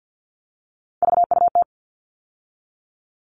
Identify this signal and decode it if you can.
Morse code, heard as a keyed tone, about 50 wpm, 700 Hz. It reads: VUT
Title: All Cut Numbers - 3-Digit 50wpm